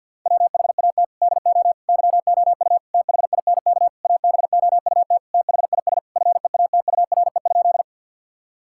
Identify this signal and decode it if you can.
Morse code, heard as a keyed tone, about 52 wpm, 700 Hz. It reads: WHAT DO YOU THINK ABOUT THIS FEATURE?
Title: Sentences from Top 1000 Words - Review Part 5 - 50wpm